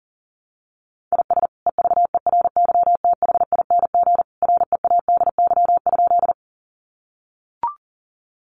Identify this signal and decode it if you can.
Morse code, heard as a keyed tone, about 40 wpm, 700 Hz. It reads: IS EVERYTHING READY?
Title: Sentences from Top 700 Words 40wpm